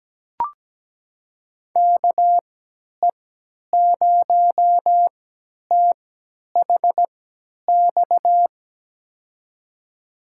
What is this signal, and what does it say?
Morse code, heard as a keyed tone, about 17 wpm, spaced as for 10 wpm, 700 Hz. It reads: KE0THX
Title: USA Call Signs - 6 Characters - 3x Character Spacing - 17wpm